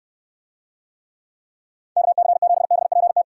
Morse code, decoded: KC7BPT

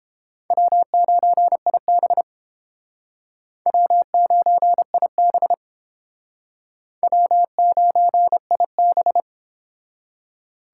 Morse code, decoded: W9IB W9IB W9IB

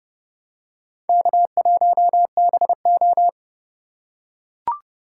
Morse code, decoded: K1BO

